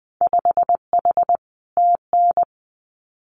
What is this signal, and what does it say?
Morse code, heard as a keyed tone, about 20 wpm, 700 Hz. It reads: 5H TN
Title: Winter Field Day Exchanges Encore 20wpm